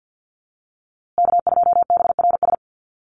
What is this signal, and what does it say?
Morse code, heard as a keyed tone, about 50 wpm, 700 Hz. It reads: K2BRS
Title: USA Call Signs Repeated 2x 50wpm